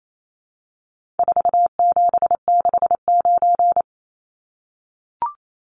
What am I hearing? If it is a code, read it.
Morse code, heard as a keyed tone, about 28 wpm, 700 Hz. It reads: 4769
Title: Numbers - 4-Digit 28wpm